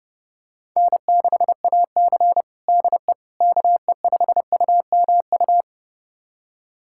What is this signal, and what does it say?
Morse code, heard as a keyed tone, about 30 wpm, 700 Hz. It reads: N6AC DE KE5UMU
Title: USA Callsign DE Callsign 30wpm